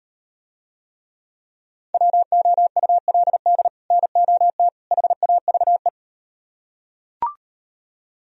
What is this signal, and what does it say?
Morse code, heard as a keyed tone, about 38 wpm, 700 Hz. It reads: WOULD NOT HAVE